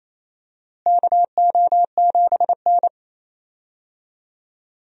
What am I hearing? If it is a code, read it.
Morse code, heard as a keyed tone, about 28 wpm, 700 Hz. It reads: KO7N
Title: USA Call Signs - 4 Characters - Repeated 3x - 28wpm